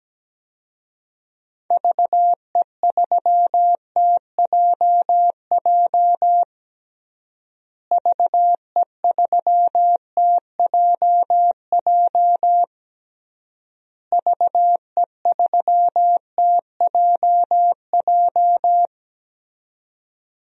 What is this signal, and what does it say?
Morse code, heard as a keyed tone, about 17 wpm, 700 Hz. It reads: VE3TJJ VE3TJJ VE3TJJ
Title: DXCC-100 Call Signs - 6 Characters - Repeated 3x - 17wpm